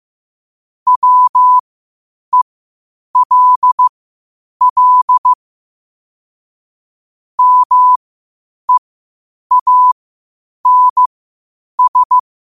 Morse code, heard as a keyed tone, about 15 wpm, spaced as for 8 wpm, 1000 Hz. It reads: WELL MEANS
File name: qotdfiles / 8wpm